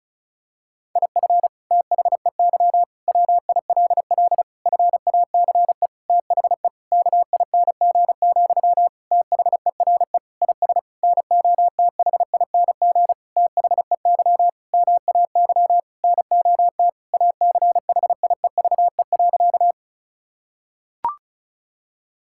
Morse code, heard as a keyed tone, about 36 wpm, 700 Hz. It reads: IF THEY WILL FACE THE KING, THERE IS NOTHING THEY MAY NOT ACHIEVE.
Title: Sentences from Top 1000 Words - Review Part 6 - 35wpm